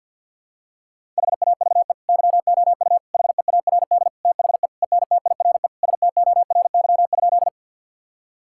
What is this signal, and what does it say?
Morse code, heard as a keyed tone, about 52 wpm, 700 Hz. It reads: HAVE YOU HEARD THE ENTIRE STORY?
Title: Sentences from Top 1000 Words - Review Part 5 - 50wpm